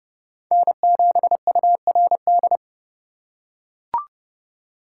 Morse code, heard as a keyed tone, about 29 wpm, 700 Hz. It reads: N7URD